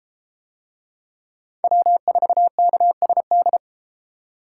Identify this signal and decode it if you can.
Morse code, heard as a keyed tone, about 32 wpm, 700 Hz. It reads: W4KSD